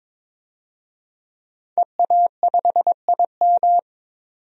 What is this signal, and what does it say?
Morse code, heard as a keyed tone, about 21 wpm, 700 Hz. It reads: EA5IM
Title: DXCC-100 Call Signs - 5 Characters - 22wpm